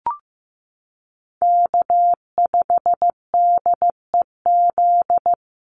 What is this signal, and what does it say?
Morse code, heard as a keyed tone, about 15 wpm, 700 Hz. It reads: K5DEZ
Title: NA SOTA Chaser Callsigns 15wpm